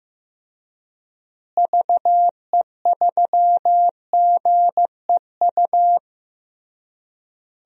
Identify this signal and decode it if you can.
Morse code, heard as a keyed tone, about 16 wpm, 700 Hz. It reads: VE3GEU
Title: DXCC-100 Call Signs - 6 Characters - Repeated 3x - 15wpm